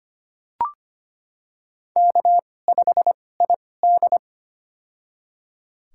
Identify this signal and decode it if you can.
Morse code, heard as a keyed tone, about 25 wpm, spaced as for 18 wpm, 700 Hz. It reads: K5ID
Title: USA Call Signs - 4 Characters - 2x Character Spacing - 25wpm